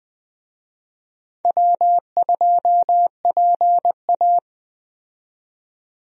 Morse code, heard as a keyed tone, about 21 wpm, 700 Hz. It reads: W2PA
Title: USA Call Signs - 4 Characters - 20wpm